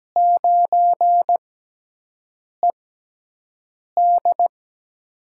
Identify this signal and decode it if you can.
Morse code, heard as a keyed tone, about 17 wpm, spaced as for 6 wpm, 700 Hz. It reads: 9ED
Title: USA Call Signs - 4 Characters - 6x Character Spacing - 17wpm